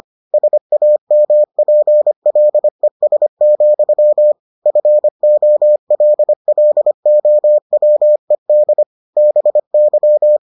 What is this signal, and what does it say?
Morse code, heard as a keyed tone, about 25 wpm, 600 Hz. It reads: SAMPLES, FOLLOWED BY